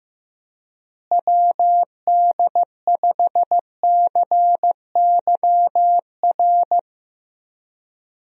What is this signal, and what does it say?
Morse code, heard as a keyed tone, about 15 wpm, 700 Hz. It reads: WD5CYR
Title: USA Call Signs - 6 Characters - Repeated 3x - 15wpm